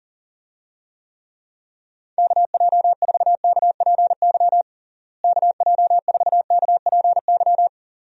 Morse code, heard as a keyed tone, about 40 wpm, 700 Hz. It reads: KJ4KPY KJ4KPY
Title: USA Call Signs - 6 Characters - Repeated 2x - 40wpm